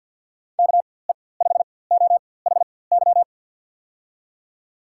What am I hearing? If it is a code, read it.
Morse code, heard as a keyed tone, about 49 wpm, spaced as for 23 wpm, 700 Hz. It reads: KE5OHY
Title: USA Call Signs - 6 Characters - 4x Character Spacing - 50wpm